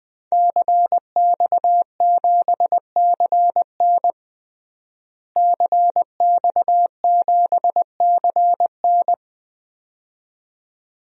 Morse code, heard as a keyed tone, about 20 wpm, 700 Hz. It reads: CX7CN CX7CN
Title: DXCC-100 Call Signs - 5 Characters - Repeated 2x - 20wpm